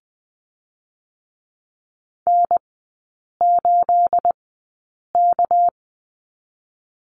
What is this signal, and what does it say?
Morse code, heard as a keyed tone, about 20 wpm, spaced as for 8 wpm, 700 Hz. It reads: N8K
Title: 218 Lesson 35e - Number 0 - Three Characters 20wpm